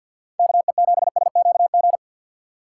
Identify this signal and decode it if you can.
Morse code, heard as a keyed tone, about 53 wpm, 700 Hz. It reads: KE7SQG
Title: USA Call Signs - 6 Characters - 6x Character Spacing - 50wpm